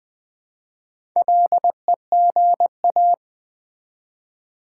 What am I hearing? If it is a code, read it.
Morse code, heard as a keyed tone, about 20 wpm, 700 Hz. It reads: LEGA
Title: Build A Word - 1 Letter At A Time - 5-Letter Words - 20wpm